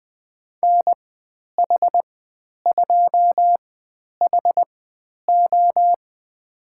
Morse code, heard as a keyed tone, about 20 wpm, spaced as for 10 wpm, 700 Hz. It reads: NH2HO
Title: 095 Lesson 16g - Letter c - Call Signs 20wpm